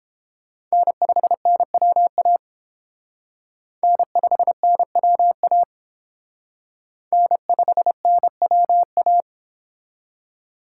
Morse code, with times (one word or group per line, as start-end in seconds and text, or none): 0.7-2.4 N5NWA
3.8-5.6 N5NWA
7.1-9.2 N5NWA